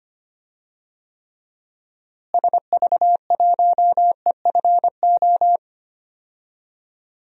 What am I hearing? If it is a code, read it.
Morse code, heard as a keyed tone, about 25 wpm, 700 Hz. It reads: SV1EFO
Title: DXCC-100 Call Signs - 6 Characters - Repeated 3x - 25wpm